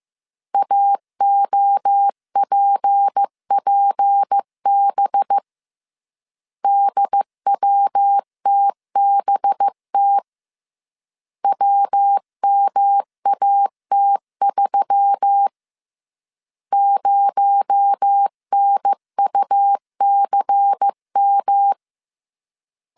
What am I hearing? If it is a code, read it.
Morse code, heard as a keyed tone, about 15 wpm, 800 Hz. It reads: AOPPB DWTBT WMAT3 0NUCM
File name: CW / DET / 1 6 1500C S B 14wpm 2958